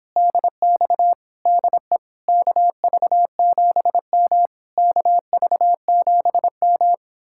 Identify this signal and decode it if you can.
Morse code, heard as a keyed tone, about 29 wpm, 700 Hz. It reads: DX DE KV7M KV7M